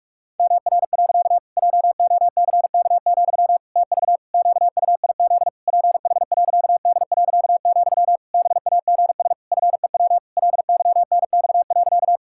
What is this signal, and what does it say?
Morse code, heard as a keyed tone, about 46 wpm, 700 Hz. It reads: MR. JOCK, TV QUIZ PH.D., BAGS FEW LYNX.
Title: Pangram Challenge v2 - 45wpm